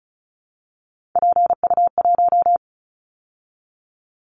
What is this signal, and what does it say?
Morse code, heard as a keyed tone, about 34 wpm, 700 Hz. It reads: PU1
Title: Callsign Prefixes for Top 50 DXCC - 35wpm